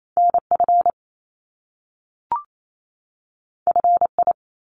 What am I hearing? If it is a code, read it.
Morse code, heard as a keyed tone, about 28 wpm, 700 Hz. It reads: NF FI
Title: Top Two Letter Combinations in Words 28wpm